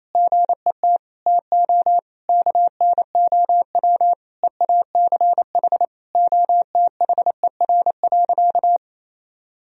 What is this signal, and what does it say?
Morse code, heard as a keyed tone, about 28 wpm, 700 Hz. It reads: GET TO KNOW EACH OTHER.
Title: Sentences from Top 1000 Words - Review Part 3 - 28wpm